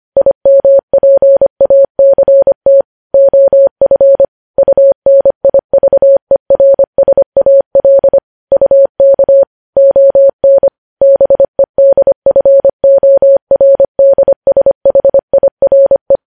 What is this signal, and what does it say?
Morse code, heard as a keyed tone, about 25 wpm, 550 Hz. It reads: IMPACT OF UNIVERSAL UK ON BEDFORDSHIRE
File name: podcast / News headlines-1x-25x25